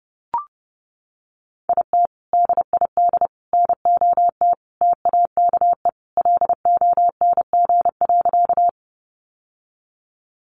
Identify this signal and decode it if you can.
Morse code, heard as a keyed tone, about 30 wpm, 700 Hz. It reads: IT DID NOT TAKE LONG.
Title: Sentences from Top 600 Words 30wpm